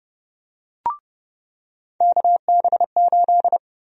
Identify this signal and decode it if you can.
Morse code, heard as a keyed tone, about 31 wpm, 700 Hz. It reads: KB8
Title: USA Call Sign Prefixes - 30wpm